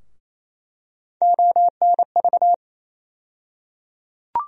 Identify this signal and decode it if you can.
Morse code, heard as a keyed tone, about 28 wpm, 700 Hz. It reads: ONV
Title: USA Call Sign Postfixes - 3 Letters 28wpm